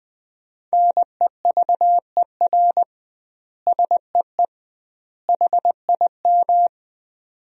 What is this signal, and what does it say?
Morse code, heard as a keyed tone, about 19 wpm, 700 Hz. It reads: NEVER SEE HIM
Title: 2x Word-Spacing - Sentences from Top 300 Words - 20wpm